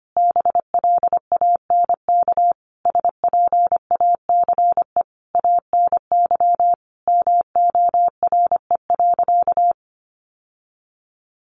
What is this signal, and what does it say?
Morse code, heard as a keyed tone, about 25 wpm, 700 Hz. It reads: BLANK SPACE ANY MORE.
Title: Sentences from Top 1500 Words 25wpm